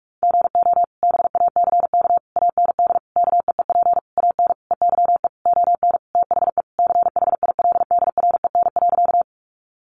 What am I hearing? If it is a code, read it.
Morse code, heard as a keyed tone, about 44 wpm, 700 Hz. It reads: GO BACK AND KEEP AN EYE ON THE CHILDREN.
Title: Sentences from Top 300 Words 45wpm